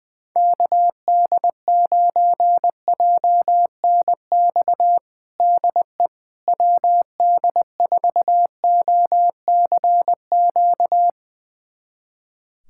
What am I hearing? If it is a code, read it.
Morse code, heard as a keyed tone, about 20 wpm, 700 Hz. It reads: KD9JNX DE WD4OCQ
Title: USA Callsign DE Callsign 20wpm